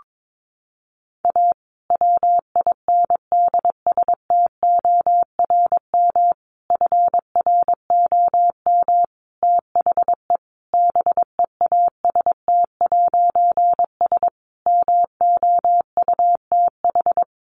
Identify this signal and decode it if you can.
Morse code, heard as a keyed tone, about 22 wpm, 700 Hz. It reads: A WINDSTORM FROM THE BEAST'S MOUTH
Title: A Sound Of Thunder 22wpm